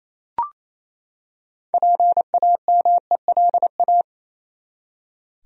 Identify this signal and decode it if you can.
Morse code, heard as a keyed tone, about 28 wpm, 700 Hz. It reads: PAMELA
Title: USA Common Names - 28wpm